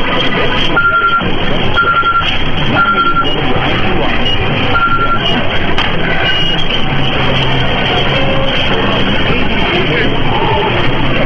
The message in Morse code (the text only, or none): none